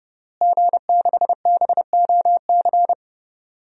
Morse code, heard as a keyed tone, about 30 wpm, 700 Hz. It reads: G6BOC
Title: DXCC-100 Call Signs - 5 Characters - Repeated 3x - 30wpm